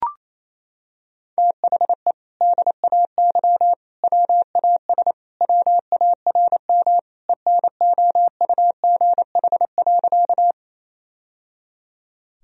Morse code, heard as a keyed tone, about 27 wpm, 700 Hz. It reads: THE DAY WAS WARM ENOUGH.
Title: Sentences from Top 1000 Words - Review Part 2 - 28wpm